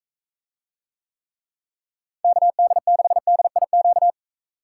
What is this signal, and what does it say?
Morse code, heard as a keyed tone, about 44 wpm, 700 Hz. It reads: KD6DIQ